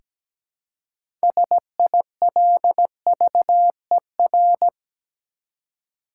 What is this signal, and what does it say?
Morse code, heard as a keyed tone, about 17 wpm, 700 Hz. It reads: SILVER
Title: Six Letter Words - 17wpm